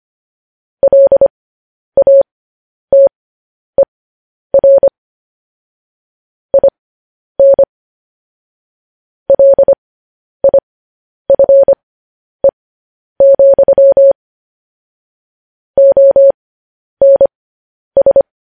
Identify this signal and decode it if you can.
Morse code, heard as a keyed tone, about 25 wpm, spaced as for 10 wpm, 550 Hz. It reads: LATER IN LIFE, ONS
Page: news headlines in morse code at 10 wpm